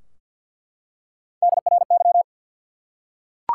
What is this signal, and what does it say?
Morse code, heard as a keyed tone, about 48 wpm, 700 Hz. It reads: DRY
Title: Three Letter Words - 50wpm